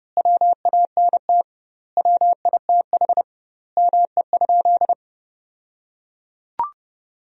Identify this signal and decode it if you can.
Morse code, heard as a keyed tone, about 32 wpm, 700 Hz. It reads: WANT WITH ME?
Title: 2x Word-Spacing - Sentences from Top 200 Words - 30wpm